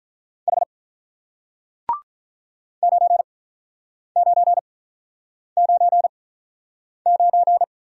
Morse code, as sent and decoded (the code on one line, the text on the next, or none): .... / ----. ----. ----. ----.
H 9999